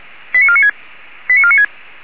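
Five-tone selective call (ZVEI), heard as quick runs of five tones, tones of about 70 ms. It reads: 89496 89496
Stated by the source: Five-tone ZVEI selective call, SDR received, from feuerwehr-orlamuende.org fire brigade audio archive.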